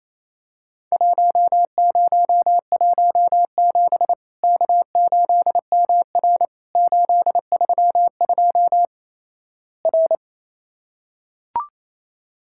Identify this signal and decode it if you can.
Morse code, heard as a keyed tone, about 28 wpm, 700 Hz. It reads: 1017 K8MR 832 R
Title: Worked All Europe DX Contest - QTC Exchange - 28wpm